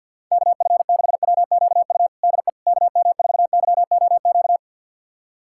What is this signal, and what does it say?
Morse code, heard as a keyed tone, about 49 wpm, 700 Hz. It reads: KF6WQU DE KM4YOQ
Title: USA Callsign DE Callsign - 6 Characters - 50wpm